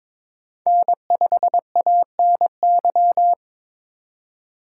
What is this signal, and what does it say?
Morse code, heard as a keyed tone, about 22 wpm, 700 Hz. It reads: N5ANY
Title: USA Call Signs - 3x Character Spacing - 22wpm